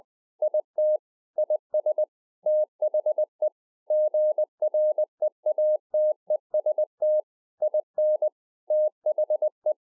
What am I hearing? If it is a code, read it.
Morse code, heard as a keyed tone, about 20 wpm, 610 Hz. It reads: IT IS THE GREATEST IN THE